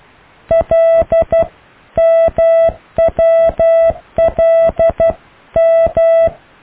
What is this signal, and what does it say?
Morse code, heard as a keyed tone, about 12 wpm, 660 Hz. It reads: L M W L M
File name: cw / IJLMU